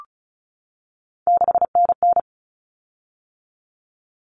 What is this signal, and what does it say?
Morse code, heard as a keyed tone, about 34 wpm, 700 Hz. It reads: 6NN
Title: Common Cut Numbers - 3-Digit 35wpm